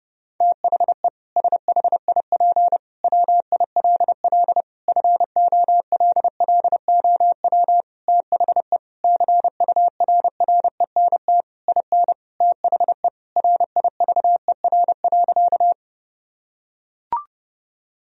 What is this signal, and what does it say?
Morse code, heard as a keyed tone, about 30 wpm, 700 Hz. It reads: THE SHIP WILL FOLLOW THE CURRENT IN THE RIVER.
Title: Sentences from Top 1000 Words - Review Part 4 - 30wpm